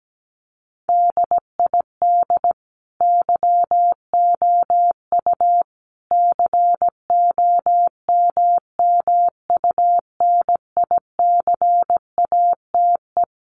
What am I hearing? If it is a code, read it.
Morse code, heard as a keyed tone, about 17 wpm, 700 Hz. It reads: DID YOU COMMUNICATE
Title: Sentences from Top 1300 Words 17wpm